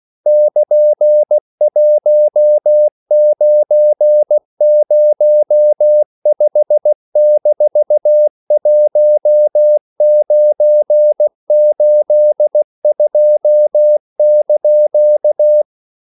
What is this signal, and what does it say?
Morse code, heard as a keyed tone, about 16 wpm, 600 Hz. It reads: <KN>1905-1982)
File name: CW / Practice / EnglishQuotations0116